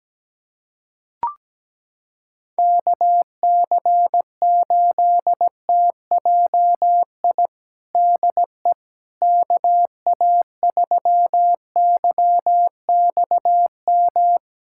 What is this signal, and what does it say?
Morse code, heard as a keyed tone, about 17 wpm, 700 Hz. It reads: KC8TJI DE KA3YXM
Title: USA Callsign DE Callsign - 6 Characters - 17wpm